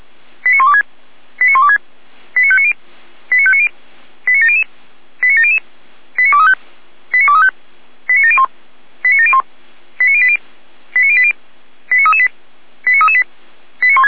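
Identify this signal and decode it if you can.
Five-tone selective call (ZVEI), heard as quick runs of five tones, tones of about 70 ms. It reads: 89126 89126 89590 89590 89700 89700 89235 89235 89701 89701 89080 89080 89338 89338
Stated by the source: ZVEI five-tone selective call received on an SDR.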